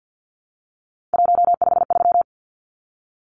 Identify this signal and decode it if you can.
Morse code, heard as a keyed tone, about 50 wpm, 700 Hz. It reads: 153
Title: Numbers - 3-Digit 50wpm